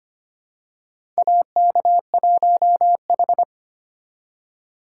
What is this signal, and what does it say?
Morse code, heard as a keyed tone, about 25 wpm, 700 Hz. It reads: AK1H
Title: USA Call Signs - 4 Characters - 4x Character Spacing - 25wpm